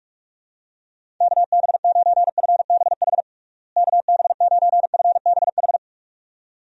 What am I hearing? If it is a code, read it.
Morse code, heard as a keyed tone, about 46 wpm, 700 Hz. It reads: KB9FBH KB9FBH
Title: USA Call Signs - 6 Characters - Repeated 2x - 45wpm